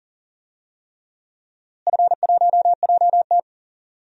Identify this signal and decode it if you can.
Morse code, heard as a keyed tone, about 40 wpm, 700 Hz. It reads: F1JT
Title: DXCC-100 Call Signs - 4 Characters - Repeated 3x - 40wpm